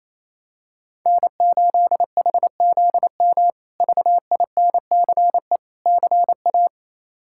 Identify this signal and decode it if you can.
Morse code, heard as a keyed tone, about 28 wpm, 700 Hz. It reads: N8HZM VINCE CA
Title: K1USN SST QSO Practice - 28wpm